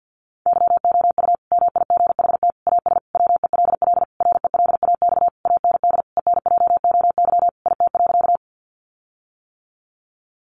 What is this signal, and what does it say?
Morse code, heard as a keyed tone, about 50 wpm, 700 Hz. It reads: YOU MIGHT AS WELL RELAX AND ENJOY IT.
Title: Sentences from Top 1400 Words 50wpm